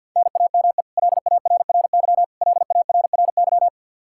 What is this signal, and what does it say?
Morse code, heard as a keyed tone, about 49 wpm, 700 Hz. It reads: NAME LARRY LARRY
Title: Practice QSO - Exchange 1 - Beginning - 50wpm